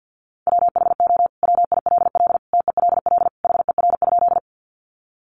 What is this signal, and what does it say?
Morse code, heard as a keyed tone, about 50 wpm, 700 Hz. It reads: WHO WILL TELL HER?
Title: Sentences from Top 200 Words 50wpm